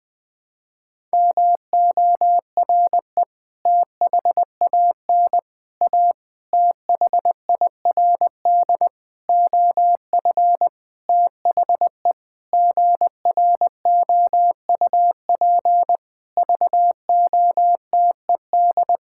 Morse code, heard as a keyed tone, about 20 wpm, 700 Hz. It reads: MORE THAN A THIRD OF THE GROUP VOTED
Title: Sentences with 6-Word Phrases - Spoken Beginning - 20wpm